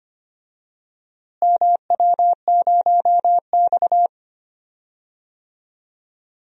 Morse code, heard as a keyed tone, about 25 wpm, 700 Hz. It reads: MW0X
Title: DXCC-100 Call Signs - 4 Characters - Repeated 2x - 25wpm